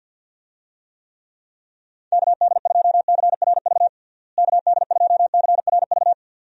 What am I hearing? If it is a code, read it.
Morse code, heard as a keyed tone, about 50 wpm, 700 Hz. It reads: KD2CRV KD2CRV
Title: USA Call Signs - 6 Characters - Repeated 2x - 50wpm